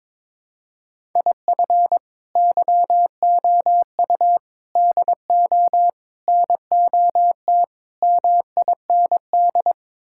Morse code, heard as a keyed tone, about 22 wpm, 700 Hz. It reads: IF YOU DO NOT MIND
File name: practice / 22wpm / Build A Phrase - 5-Word Phrases - 22wpm